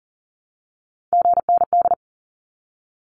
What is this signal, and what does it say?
Morse code, heard as a keyed tone, about 40 wpm, 700 Hz. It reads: GND
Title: Abbreviations 40wpm